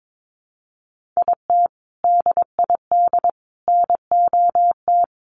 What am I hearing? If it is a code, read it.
Morse code, heard as a keyed tone, about 22 wpm, 700 Hz. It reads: IT DID NOT
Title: Sets of 3 Words 22wpm